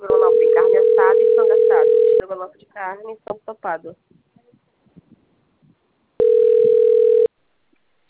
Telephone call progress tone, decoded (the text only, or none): ringback tone